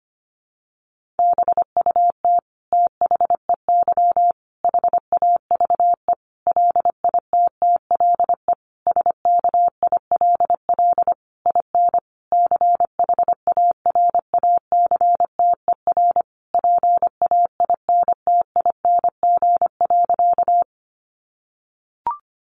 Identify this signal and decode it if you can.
Morse code, heard as a keyed tone, about 25 wpm, 700 Hz. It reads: BUT THEY HAVE LITTLE SKILL IN CHARACTER PAINTING.
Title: Sentences from Top 1200 Words 25wpm